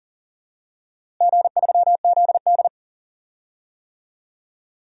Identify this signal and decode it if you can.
Morse code, heard as a keyed tone, about 40 wpm, 700 Hz. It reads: G3ZD
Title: DXCC-100 Call Signs - 4 Characters - Repeated 3x - 40wpm